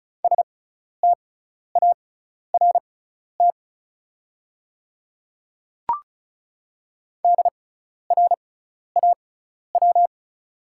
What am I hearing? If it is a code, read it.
Morse code, heard as a keyed tone, about 34 wpm, spaced as for 12 wpm, 690 Hz. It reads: START DRAW
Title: 6x Character Spacing - Top 300 Words - 35wpm